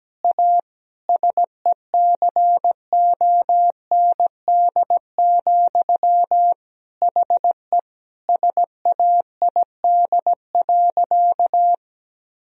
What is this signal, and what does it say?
Morse code, heard as a keyed tone, about 17 wpm, 700 Hz. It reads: A SECOND, HE SAID.
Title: Sentences from Top 1000 Words - Review Part 1 - 17wpm